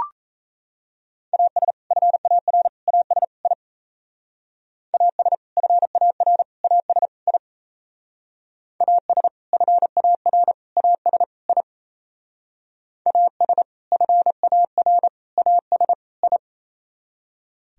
Morse code, heard as a keyed tone, about 35 wpm, 700 Hz. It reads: AS FAR AS I AS FAR AS I AS FAR AS I AS FAR AS I
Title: Speed-Racing - Sets of 4 Words 28wpm